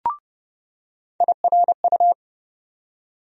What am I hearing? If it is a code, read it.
Morse code, heard as a keyed tone, about 30 wpm, 700 Hz. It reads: IRU
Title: USA Call Sign Postfixes - 3 Letters 30wpm